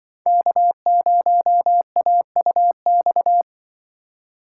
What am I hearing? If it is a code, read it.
Morse code, heard as a keyed tone, about 25 wpm, 700 Hz. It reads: K0AUX
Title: Speed Racing - USA 1x3 Call Signs 20wpm